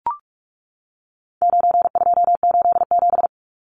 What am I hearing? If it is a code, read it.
Morse code, heard as a keyed tone, about 44 wpm, 700 Hz. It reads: 9287
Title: Numbers - 4-Digit 45wpm